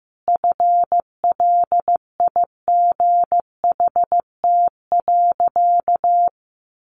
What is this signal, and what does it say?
Morse code, heard as a keyed tone, about 15 wpm, 700 Hz. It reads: FLIGHT.
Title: Sentences from Top 1400 Words 15wpm